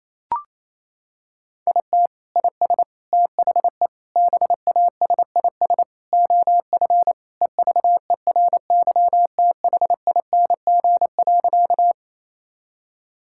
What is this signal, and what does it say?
Morse code, heard as a keyed tone, about 28 wpm, 700 Hz. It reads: IT IS THE BASIS OF EVERYTHING.
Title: Sentences from Top 1000 Words - Review Part 5 - 28wpm